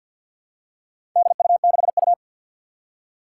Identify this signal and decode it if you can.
Morse code, heard as a keyed tone, about 52 wpm, 700 Hz. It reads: DU6U
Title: Four Challenging Letter-Number Combinations 50wpm